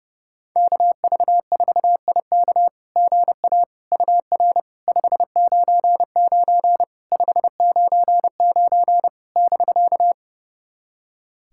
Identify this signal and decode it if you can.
Morse code, heard as a keyed tone, about 30 wpm, 690 Hz. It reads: KV4IK GA UR 599 599 <BK>
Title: SOTA QSO Practice 30wpm